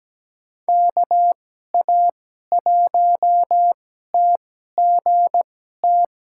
Morse code, heard as a keyed tone, about 17 wpm, spaced as for 12 wpm, 700 Hz. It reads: KA1TGT
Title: USA Call Signs - 6 Characters - 2x Character Spacing - 17wpm